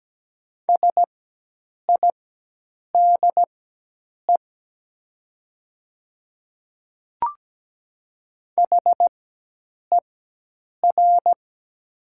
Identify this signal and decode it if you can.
Morse code, heard as a keyed tone, about 17 wpm, spaced as for 8 wpm, 700 Hz. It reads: SIDE HER
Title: 4x Character Spacing - Top 100 Words - 17wpm